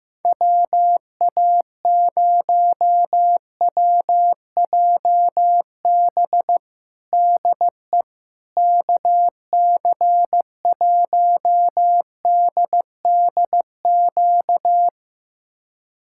Morse code, heard as a keyed tone, about 15 wpm, 700 Hz. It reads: WA0WJB DE KC1DDQ
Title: USA Callsign DE Callsign - 6 Characters - 15wpm